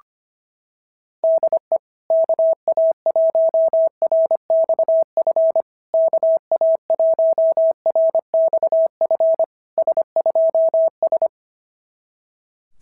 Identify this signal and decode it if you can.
Morse code, heard as a keyed tone, about 25 wpm, 660 Hz. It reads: DE KA1RXF KA1RXF S2S
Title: SOTA QSO Practice 25wpm